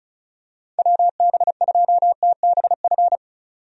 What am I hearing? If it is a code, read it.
Morse code, heard as a keyed tone, about 34 wpm, 700 Hz. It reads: WB2TBF
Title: USA Call Signs - 6 Characters - 35wpm